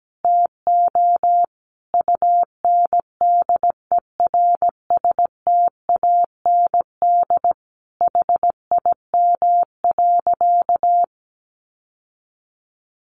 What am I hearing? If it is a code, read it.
Morse code, heard as a keyed tone, about 17 wpm, 700 Hz. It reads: TO UNDERSTAND HIM.
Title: Sentences from Top 600 Words 17wpm